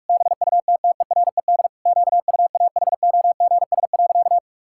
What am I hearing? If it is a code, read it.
Morse code, heard as a keyed tone, about 46 wpm, 700 Hz. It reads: BUTTERED QUAHOGS.
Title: Pangram Challenge v2 - 45wpm